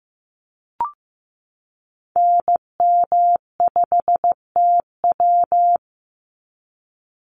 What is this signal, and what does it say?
Morse code, heard as a keyed tone, about 15 wpm, 700 Hz. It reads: NM5TW
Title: NA SOTA Activator Callsigns 15wpm